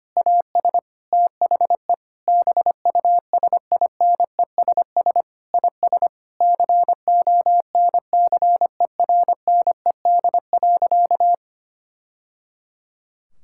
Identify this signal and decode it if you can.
Morse code, heard as a keyed tone, about 25 wpm, 700 Hz. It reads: AS THE BUSINESS IS CONCERNED.
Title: Sentences from Top 1000 Words - Review Part 6 - 25wpm